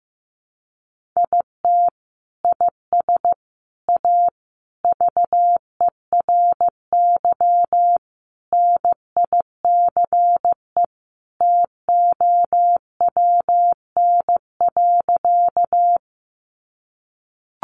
Morse code, heard as a keyed tone, about 15 wpm, 700 Hz. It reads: IT IS A VERY NICE TOWN.
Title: Sentences from Top 1100 Words 15wpm